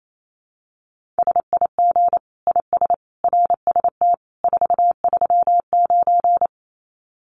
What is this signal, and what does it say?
Morse code, heard as a keyed tone, about 28 wpm, 700 Hz. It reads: SIG IS RST 439
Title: Common QSO Phrases 28wpm